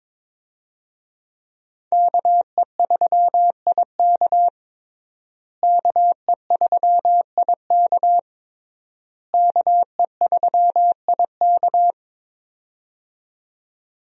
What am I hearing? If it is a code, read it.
Morse code, heard as a keyed tone, about 22 wpm, 700 Hz. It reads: KE3IK KE3IK KE3IK
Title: USA Call Signs - 5 Characters - Repeated 3x - 22wpm